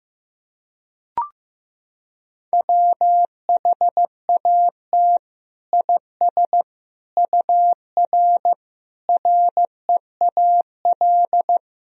Morse code, heard as a keyed tone, about 15 wpm, 700 Hz. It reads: WHAT IS UR REAL
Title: Sentences from Top 300 Words and Abbreviations 15wpm